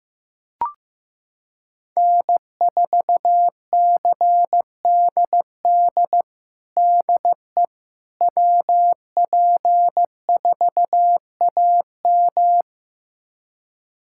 Morse code, heard as a keyed tone, about 15 wpm, 700 Hz. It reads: N4CDD DE WP4AM
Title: USA Callsign DE Callsign - 5 Characters - 15wpm